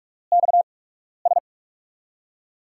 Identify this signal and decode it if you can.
Morse code, heard as a keyed tone, about 45 wpm, spaced as for 12 wpm, 700 Hz. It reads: XS
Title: USA Call Signs - 5 Characters - 8x Character Spacing - 45wpm